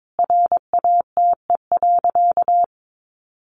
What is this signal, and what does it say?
Morse code, heard as a keyed tone, about 22 wpm, 700 Hz. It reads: RATE.
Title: Sentences from Top 700 Words 22wpm